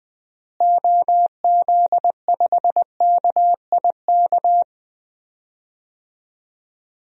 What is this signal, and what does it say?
Morse code, heard as a keyed tone, about 21 wpm, 700 Hz. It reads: OZ5KIK
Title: DXCC-100 Call Signs - 6 Characters - Repeated 2x - 20wpm